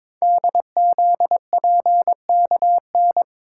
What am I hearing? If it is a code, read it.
Morse code, heard as a keyed tone, about 22 wpm, 700 Hz. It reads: DZPKN
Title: Code Groups - 5 Letters - 22wpm